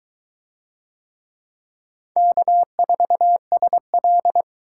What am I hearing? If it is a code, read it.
Morse code, heard as a keyed tone, about 22 wpm, 700 Hz. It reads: K4SL